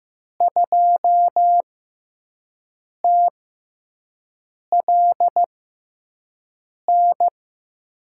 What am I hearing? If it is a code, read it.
Morse code, heard as a keyed tone, about 15 wpm, spaced as for 5 wpm, 700 Hz. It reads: 2TLN
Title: USA Call Signs - 6x Character Spacing - 15wpm